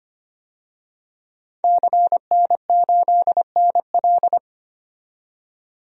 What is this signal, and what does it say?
Morse code, heard as a keyed tone, about 25 wpm, 700 Hz. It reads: CN8NL
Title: DXCC-100 Call Signs - 5 Characters - 25wpm